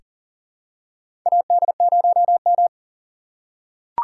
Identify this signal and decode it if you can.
Morse code, heard as a keyed tone, about 40 wpm, 700 Hz. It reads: AD0M
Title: USA Call Signs - 3x Character Spacing - 40wpm